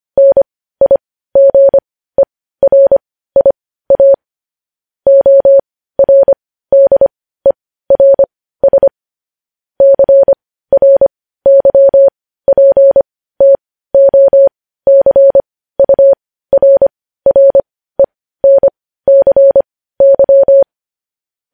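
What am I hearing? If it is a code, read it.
Morse code, heard as a keyed tone, about 25 wpm, spaced as for 15 wpm, 550 Hz. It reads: NIGERIA ORDERS CRYPTOCURRENCY
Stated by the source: Morse code news-headline practice sent at 15 words per minute.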